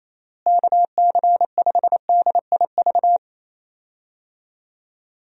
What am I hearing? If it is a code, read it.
Morse code, heard as a keyed tone, about 27 wpm, 700 Hz. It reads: KC5DIV
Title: USA Call Signs - 6 Characters - Repeated 2x - 28wpm